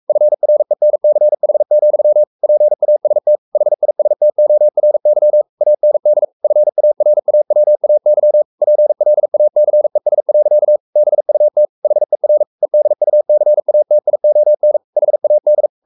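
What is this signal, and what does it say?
Morse code, heard as a keyed tone, about 44 wpm, 600 Hz. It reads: FRENCH, PAST HISTORY AND FARAWAY PLACES. BUT HER EDUCATION HAD
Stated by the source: Morse code at 43 wpm sending Ken Follett's novel Never in English, from CW med Gnister.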